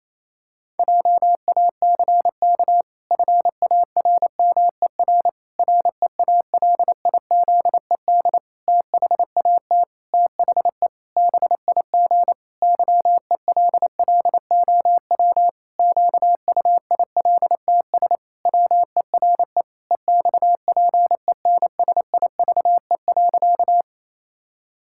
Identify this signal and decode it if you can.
Morse code, heard as a keyed tone, about 27 wpm, 700 Hz. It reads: JACK FARMER REALIZED THAT THE BIG YELLOW QUILTS WERE EXPENSIVE.
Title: Pangram Challenge v2 - 28wpm